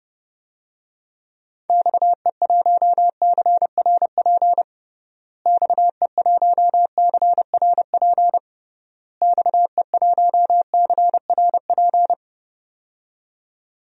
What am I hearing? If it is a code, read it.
Morse code, heard as a keyed tone, about 30 wpm, 700 Hz. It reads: XE1CRP XE1CRP XE1CRP
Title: DXCC-100 Call Signs - 6 Characters - Repeated 3x - 30wpm